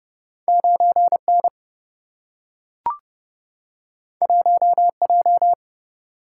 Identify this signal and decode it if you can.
Morse code, heard as a keyed tone, about 30 wpm, 700 Hz. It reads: 9 N 1 J
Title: Single-Letter Single-Number 30wpm